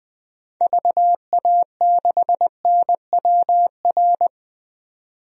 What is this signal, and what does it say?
Morse code, heard as a keyed tone, about 20 wpm, 700 Hz. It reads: VA6NWR
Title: DXCC-100 Call Signs - 6 Characters - 20wpm